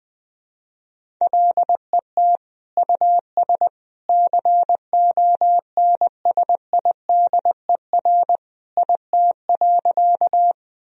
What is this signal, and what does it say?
Morse code, heard as a keyed tone, about 20 wpm, 700 Hz. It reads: LET US CONSIDER IT.
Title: Sentences from Top 1000 Words - Review Part 4 - 20wpm